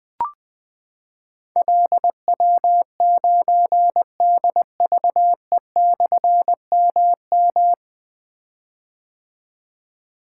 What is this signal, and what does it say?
Morse code, heard as a keyed tone, about 21 wpm, 700 Hz. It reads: LW9DVE/MM
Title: Difficult Call Signs 20wpm